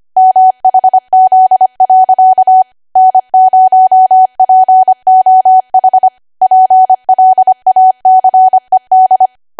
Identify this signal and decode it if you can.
Morse code, heard as a keyed tone, about 25 wpm, 750 Hz. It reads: MHZ. N0POH PLACED